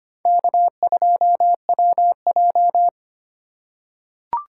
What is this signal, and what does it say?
Morse code, heard as a keyed tone, about 25 wpm, 700 Hz. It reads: K2WJ
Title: USA Call Signs - 4 Characters - 2x Character Spacing - 25wpm